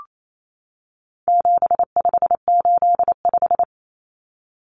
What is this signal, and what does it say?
Morse code, heard as a keyed tone, about 28 wpm, 700 Hz. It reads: 7585
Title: Common Cut Numbers - 4-Digit 28wpm